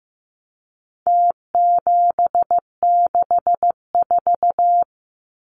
Morse code, heard as a keyed tone, about 15 wpm, 700 Hz. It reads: T764
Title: Numbers - Contest Style - 4-Digit 15wpm